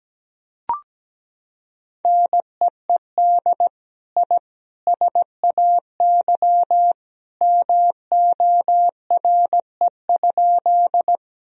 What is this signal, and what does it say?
Morse code, heard as a keyed tone, about 17 wpm, 700 Hz. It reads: NEED I SAY MORE?
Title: Sentences from Top 1000 Words - Review Part 1 - 17wpm